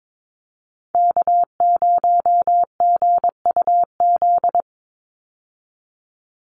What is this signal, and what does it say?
Morse code, heard as a keyed tone, about 22 wpm, 700 Hz. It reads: K0GUZ
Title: Call Signs Repeated 3x - 22wpm